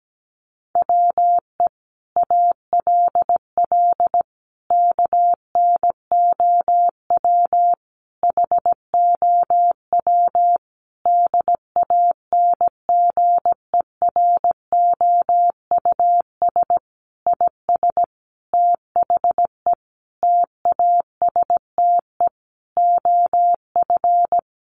Morse code, heard as a keyed tone, about 17 wpm, 700 Hz. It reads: WE ALL KNOW HOW DANGEROUS IS THE TASTE OF
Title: Sentences from Top 1200 Words 17wpm